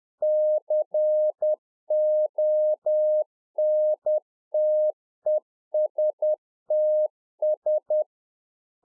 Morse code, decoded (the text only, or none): CONTESTS